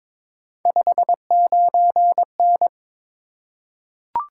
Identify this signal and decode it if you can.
Morse code, heard as a keyed tone, about 22 wpm, 700 Hz. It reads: 59N
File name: practice / 22wpm / Three Letter-Number Combinations - 22wpm